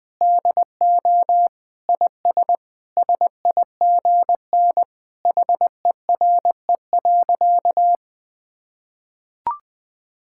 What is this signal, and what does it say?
Morse code, heard as a keyed tone, about 20 wpm, 700 Hz. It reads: DO IS SIGN HERE.
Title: Sentences with 6-Word Phrases - 20wpm